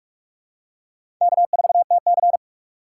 Morse code, heard as a keyed tone, about 44 wpm, 690 Hz. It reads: K4TC
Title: USA Call Signs - 4 Characters - 3x Character Spacing - 45wpm